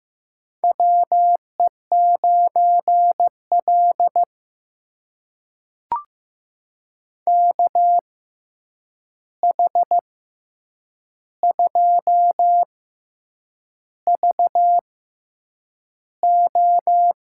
Morse code, heard as a keyed tone, about 15 wpm, spaced as for 7 wpm, 700 Hz. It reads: WE9L KH2VO
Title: USA Call Signs - 6x Character Spacing - 15wpm